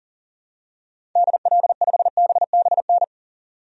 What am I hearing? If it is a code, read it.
Morse code, heard as a keyed tone, about 40 wpm, 700 Hz. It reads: DL5BBN
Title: DXCC-100 Call Signs - 6 Characters - 40wpm